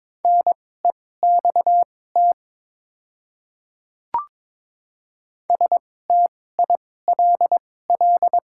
Morse code, heard as a keyed tone, about 22 wpm, spaced as for 16 wpm, 700 Hz. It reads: NEXT STILL
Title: 2x Character Spacing - Top 300 Words - 22wpm